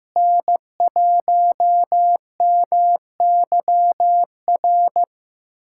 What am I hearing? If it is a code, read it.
Morse code, heard as a keyed tone, about 15 wpm, 700 Hz. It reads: N1MYR
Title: USA Callsign DE Callsign 15wpm